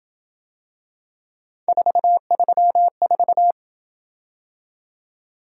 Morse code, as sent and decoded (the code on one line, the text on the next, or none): ....- ...-- ....-
434